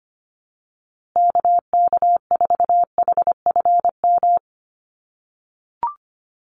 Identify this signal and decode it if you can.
Morse code, heard as a keyed tone, about 25 wpm, 700 Hz. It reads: KK4HFM
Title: USA Call Signs Repeated 3x 25wpm